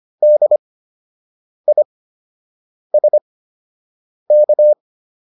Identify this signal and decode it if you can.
Morse code, heard as a keyed tone, about 25 wpm, spaced as for 7 wpm, 600 Hz. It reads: DISK